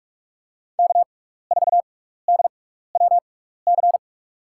Morse code, heard as a keyed tone, about 46 wpm, spaced as for 16 wpm, 700 Hz. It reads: K4DWC
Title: USA Call Signs - 5 Characters - 6x Character Spacing - 45wpm